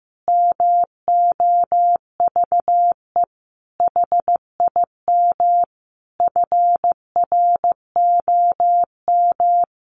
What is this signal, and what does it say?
Morse code, heard as a keyed tone, about 15 wpm, 700 Hz. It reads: MOVE HIM FROM
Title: Sentences from Top 1300 Words 15wpm